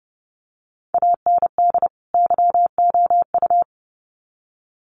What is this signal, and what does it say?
Morse code, heard as a keyed tone, about 30 wpm, 700 Hz. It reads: AND YOU
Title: Sets of 2 Words 30wpm